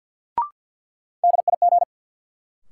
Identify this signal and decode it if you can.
Morse code, heard as a keyed tone, about 53 wpm, 700 Hz. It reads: DIG
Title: Sentences from Top 1600 Words - New Word Drill Alt - 50wpm